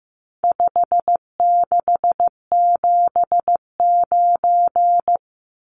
Morse code, heard as a keyed tone, about 15 wpm, 700 Hz. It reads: 5679
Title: Numbers - 4-Digit 15wpm